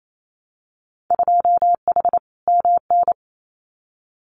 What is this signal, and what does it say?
Morse code, heard as a keyed tone, about 28 wpm, 700 Hz. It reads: 2H MN
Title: Winter Field Day Exchanges 28wpm